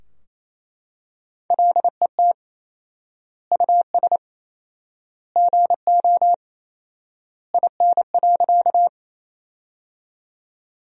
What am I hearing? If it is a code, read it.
Morse code, heard as a keyed tone, about 27 wpm, 700 Hz. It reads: LET US GO IN.
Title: 4x Word-Spacing - Sentences from Top 300 Words - 28wpm